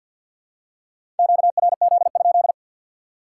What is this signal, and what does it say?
Morse code, heard as a keyed tone, about 52 wpm, 700 Hz. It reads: QRZ?
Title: Top 90 QSO Elements 50wpm